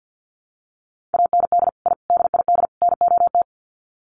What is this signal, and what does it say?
Morse code, heard as a keyed tone, about 50 wpm, 700 Hz. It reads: AND I DID NOT
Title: Sets of 4 Words 50wpm